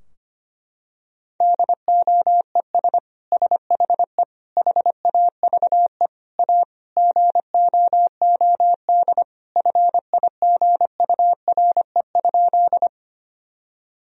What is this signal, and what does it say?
Morse code, heard as a keyed tone, about 25 wpm, 700 Hz. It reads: DOES SHE HAVE A GOOD FIGURE?
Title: Sentences from Top 1000 Words - Review Part 2 - 25wpm